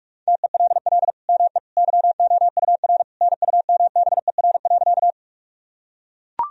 Morse code, heard as a keyed tone, about 42 wpm, 700 Hz. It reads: TELL ME YOUR NUMBER.
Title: Sentences from Top 1000 Words - Review Part 1 - 45wpm